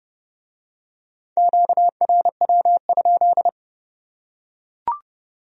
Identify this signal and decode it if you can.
Morse code, heard as a keyed tone, about 28 wpm, 700 Hz. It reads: QRW?